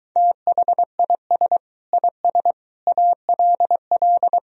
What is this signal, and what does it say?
Morse code, heard as a keyed tone, about 23 wpm, 700 Hz. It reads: THIS IS ALL